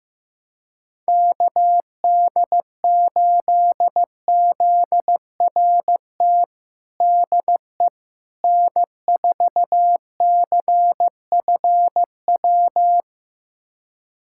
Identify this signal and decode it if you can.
Morse code, heard as a keyed tone, about 16 wpm, 700 Hz. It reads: KD8ZRT DE N4CFW
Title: USA Callsign DE Callsign 15wpm